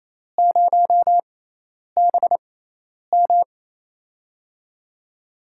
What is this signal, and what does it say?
Morse code, heard as a keyed tone, about 28 wpm, spaced as for 10 wpm, 700 Hz. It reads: 0BM
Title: USA Call Signs - 4 Characters - 6x Character Spacing - 28wpm